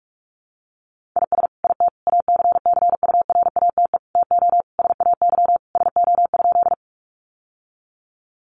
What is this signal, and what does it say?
Morse code, heard as a keyed tone, about 46 wpm, 700 Hz. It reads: IS IT ACCURATE TO SAY SO?
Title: Sentences from Top 1200 Words 45wpm